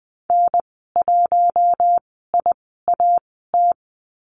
Morse code, heard as a keyed tone, about 20 wpm, spaced as for 14 wpm, 700 Hz. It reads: N1IAT
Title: 042 Lesson 8g - Number 1 - Call Signs 20wpm